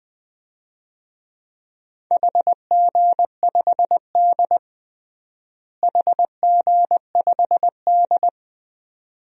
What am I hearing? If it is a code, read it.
Morse code, heard as a keyed tone, about 21 wpm, 700 Hz. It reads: HG5D HG5D
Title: DXCC-100 Call Signs - 4 Characters - Repeated 2x - 20wpm